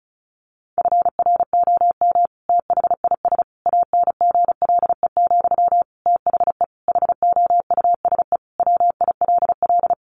Morse code, heard as a keyed tone, about 34 wpm, 700 Hz. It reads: FROM THIS ANGLE, THE HOUSE WILL
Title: Sentences from Top 1500 Words 35wpm